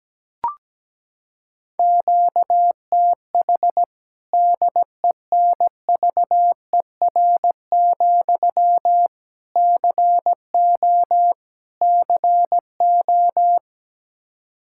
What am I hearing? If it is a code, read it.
Morse code, heard as a keyed tone, about 17 wpm, 700 Hz. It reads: QTH DENVER, CO CO
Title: Practice QSO - Exchange 1 - Beginning - 17wpm